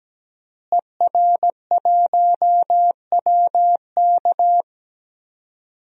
Morse code, heard as a keyed tone, about 17 wpm, 700 Hz. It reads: ER1WK
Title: DXCC-100 Call Signs - 5 Characters - Repeated 3x - 17wpm